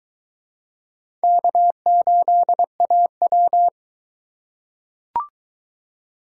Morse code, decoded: K8AW

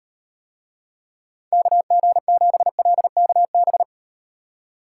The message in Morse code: KG7LKB

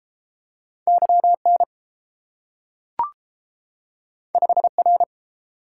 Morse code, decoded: YN 5R

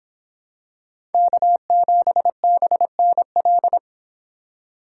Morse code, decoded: K7BNL

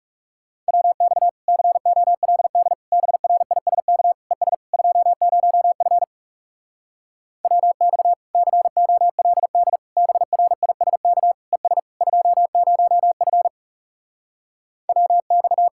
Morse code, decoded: WX COLD BRISK ES 20F WX COLD BRISK ES 20F WX

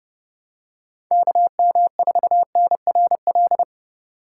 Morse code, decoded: KM4NRL

